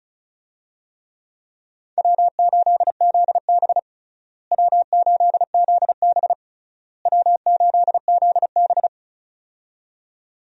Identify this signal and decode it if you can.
Morse code, heard as a keyed tone, about 34 wpm, 700 Hz. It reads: W8ZB W8ZB W8ZB
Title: USA Call Signs - 4 Characters - Repeated 3x - 35wpm